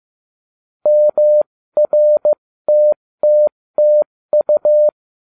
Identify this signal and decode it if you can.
Morse code, heard as a keyed tone, about 15 wpm, spaced as for 13 wpm, 600 Hz. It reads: MRTTTU